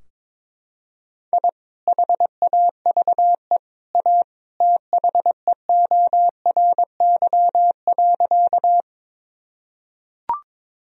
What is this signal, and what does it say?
Morse code, heard as a keyed tone, about 22 wpm, 700 Hz. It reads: I HAVE A THEORY.
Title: Sentences from Top 1000 Words - Review Part 3 - 22wpm